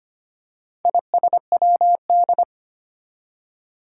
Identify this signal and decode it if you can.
Morse code, heard as a keyed tone, about 25 wpm, 700 Hz. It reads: ISWD
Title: Four Challenging Letter-Number Combinations 25wpm